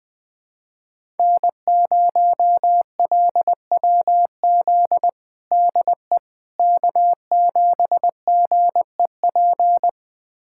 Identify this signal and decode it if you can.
Morse code, heard as a keyed tone, about 21 wpm, 700 Hz. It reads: N0LWZ DE K7GEP
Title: USA Callsign DE Callsign - 5 Characters - 20wpm